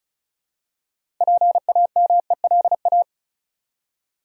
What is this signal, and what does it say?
Morse code, heard as a keyed tone, about 36 wpm, 700 Hz. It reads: PAMELA
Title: USA Common Names - 35wpm